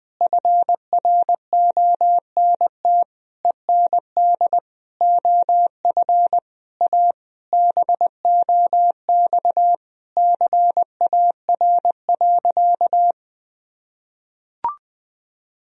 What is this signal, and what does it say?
Morse code, heard as a keyed tone, about 20 wpm, 700 Hz. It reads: FRONT END OF A BOX CAR.
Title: Sentences from Top 1000 Words - Review Part 2 - 20wpm